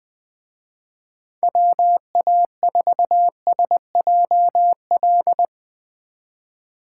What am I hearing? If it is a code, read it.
Morse code, heard as a keyed tone, about 20 wpm, 700 Hz. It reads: WA4SJL
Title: Speed Racing - USA 2x3 Call Signs 17wpm